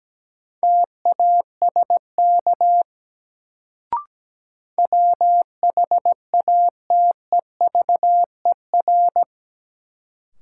Morse code, heard as a keyed tone, about 17 wpm, 700 Hz. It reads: TASK WHATEVER
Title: Sentences from Top 1100 Words - New Word Drill Alt - 17wpm